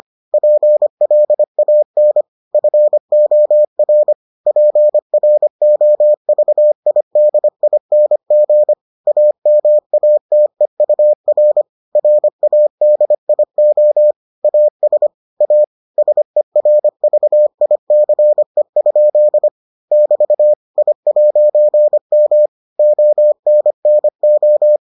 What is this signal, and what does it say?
Morse code, heard as a keyed tone, about 25 wpm, 600 Hz. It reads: PLAN FOR PROVIDING AMATEUR RADIO AS A SERVICE? <BT> I'M ONNO